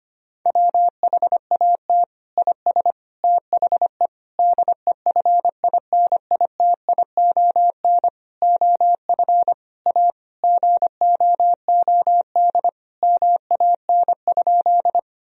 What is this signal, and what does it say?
Morse code, heard as a keyed tone, about 25 wpm, 700 Hz. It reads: WHAT IS THE DEFINITION OF A GOOD MAN?
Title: Sentences from Top 1000 Words - Review Part 4 - 25wpm